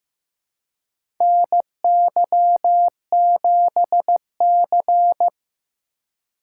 Morse code, heard as a keyed tone, about 15 wpm, 700 Hz. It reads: NY7C
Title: USA Call Signs - 4 Characters - 4x Character Spacing - 15wpm